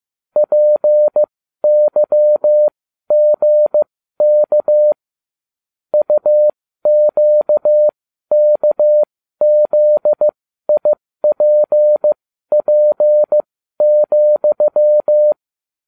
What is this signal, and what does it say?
Morse code, heard as a keyed tone, about 15 wpm, spaced as for 12 wpm, 600 Hz. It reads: PYGK UQKZIPP,